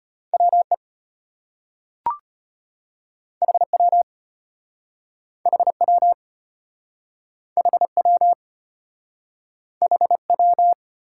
Morse code, heard as a keyed tone, about 33 wpm, 700 Hz. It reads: WE HW HW HW HW